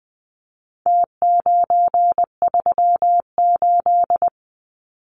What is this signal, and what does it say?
Morse code, heard as a keyed tone, about 19 wpm, 700 Hz. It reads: T938
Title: Numbers - Contest Style - 4-Digit 20wpm